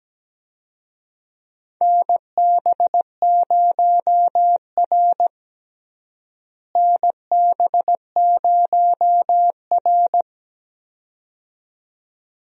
Morse code, heard as a keyed tone, about 17 wpm, 700 Hz. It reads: NB0R NB0R
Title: USA Call Signs - 4 Characters - Repeated 2x - 17wpm